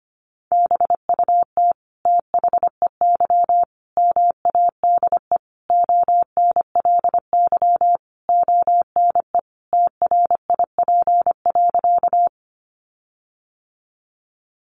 Morse code, heard as a keyed tone, about 25 wpm, 700 Hz. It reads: BUT THEY MADE ONLY ONE TRIP.
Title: Sentences from Top 1300 Words 25wpm